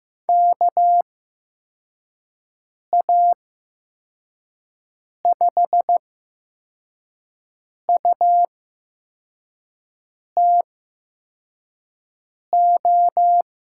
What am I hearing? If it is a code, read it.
Morse code, heard as a keyed tone, about 15 wpm, spaced as for 4 wpm, 700 Hz. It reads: KA5UTO
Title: USA Call Signs - 8x Character Spacing - 15wpm